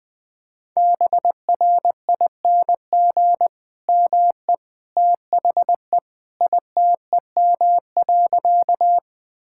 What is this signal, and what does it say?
Morse code, heard as a keyed tone, about 21 wpm, 700 Hz. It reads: BRING ME THE ITEM.
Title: Sentences from Top 1000 Words - Review Part 6 - 20wpm